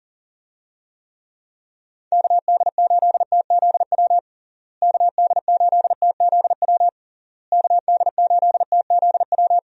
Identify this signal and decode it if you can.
Morse code, heard as a keyed tone, about 40 wpm, 700 Hz. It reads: KD8TZW KD8TZW KD8TZW
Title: USA Call Signs - 6 Characters - Repeated 3x - 40wpm